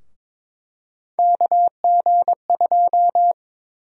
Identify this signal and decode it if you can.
Morse code, heard as a keyed tone, about 21 wpm, 700 Hz. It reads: KG2
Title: USA Call Sign Prefixes - 22wpm